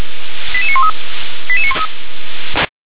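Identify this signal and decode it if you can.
Five-tone selective call (ZVEI), heard as quick runs of five tones, tones of about 70 ms. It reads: 88913
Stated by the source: ZVEI five-tone call received by SDR.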